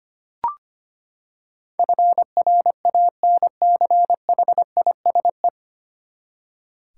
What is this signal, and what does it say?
Morse code, heard as a keyed tone, about 25 wpm, 700 Hz. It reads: FRANCHISE
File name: Nine Letter Words - 25wpm